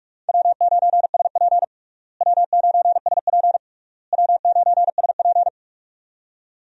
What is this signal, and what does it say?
Morse code, heard as a keyed tone, about 43 wpm, 700 Hz. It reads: W9SP W9SP W9SP
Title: USA Call Signs - 4 Characters - Repeated 3x - 45wpm